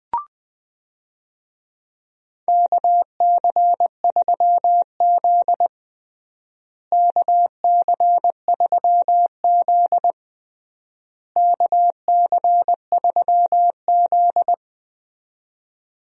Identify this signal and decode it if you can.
Morse code, heard as a keyed tone, about 20 wpm, 690 Hz. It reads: KC3Z KC3Z KC3Z
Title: USA Call Signs - 4 Characters - Repeated 3x - 20wpm